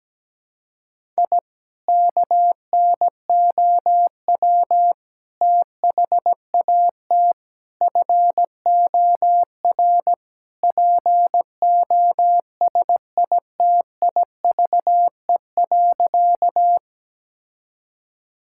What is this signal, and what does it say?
Morse code, heard as a keyed tone, about 17 wpm, 700 Hz. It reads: I KNOW THAT FOR POSITIVE.
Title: Sentences from Top 1000 Words - Review Part 5 - 17wpm